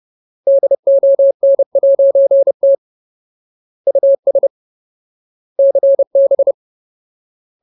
Morse code, heard as a keyed tone, about 30 wpm, 550 Hz. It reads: DON'T US CB
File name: Ultimate Word List / 30WPM Ultimate